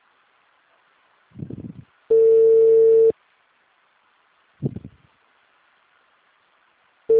Ringback tone (China, 450 Hz).
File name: out- -2026- - - .31297